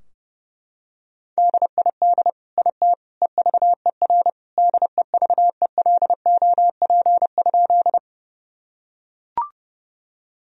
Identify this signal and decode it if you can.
Morse code, heard as a keyed tone, about 32 wpm, 700 Hz. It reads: DID IT EVER DEVELOP?
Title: Sentences from Top 1000 Words - Review Part 2 - 30wpm